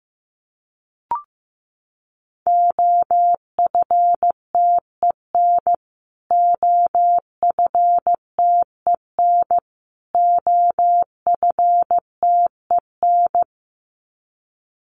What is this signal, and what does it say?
Morse code, heard as a keyed tone, about 15 wpm, 700 Hz. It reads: OFTEN OFTEN OFTEN
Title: Top 500 Words Repeated 15wpm